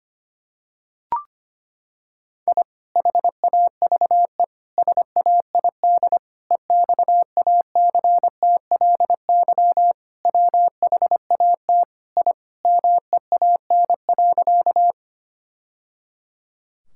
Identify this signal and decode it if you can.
Morse code, heard as a keyed tone, about 25 wpm, 700 Hz. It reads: I HAVE SAID EXACTLY WHAT I MEAN.
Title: Sentences from Top 1000 Words - Review Part 6 - 25wpm